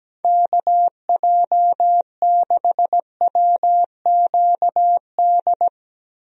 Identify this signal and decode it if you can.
Morse code, heard as a keyed tone, about 17 wpm, 700 Hz. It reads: KJ6WQD
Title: USA Call Signs - 6 Characters - 17wpm